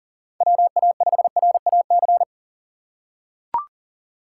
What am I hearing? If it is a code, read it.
Morse code, heard as a keyed tone, about 40 wpm, 700 Hz. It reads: WA5RAC
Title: USA Call Signs - 6 Characters - 2x Character Spacing - 40wpm